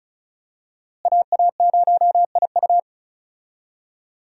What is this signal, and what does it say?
Morse code, heard as a keyed tone, about 36 wpm, 700 Hz. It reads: AA0IU
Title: USA Call Signs - 5 Characters - Repeated 2x - 35wpm